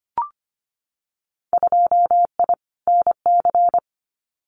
Morse code, heard as a keyed tone, about 25 wpm, 700 Hz. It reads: 2I NC
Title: Winter Field Day Exchanges 25wpm